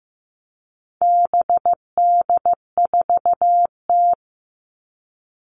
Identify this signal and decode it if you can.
Morse code, heard as a keyed tone, about 15 wpm, 700 Hz. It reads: BD4T
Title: All Cut Numbers - 4-Digit 15wpm